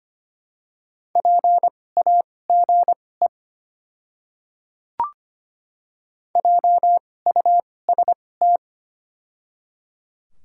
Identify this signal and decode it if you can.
Morse code, heard as a keyed tone, about 25 wpm, spaced as for 18 wpm, 700 Hz. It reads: PAGE JUST
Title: 2x Character Spacing - Top 300 Words - 25wpm